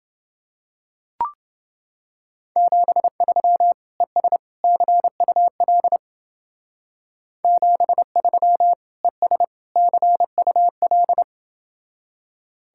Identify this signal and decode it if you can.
Morse code, heard as a keyed tone, about 30 wpm, 700 Hz. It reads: 73 ES CUL 73 ES CUL